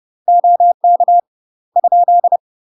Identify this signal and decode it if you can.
Morse code, heard as a keyed tone, about 30 wpm, 700 Hz. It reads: OK ?
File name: UK MASTER QSOs TRAINING / 30WPM / UK MASTER QSOs 0008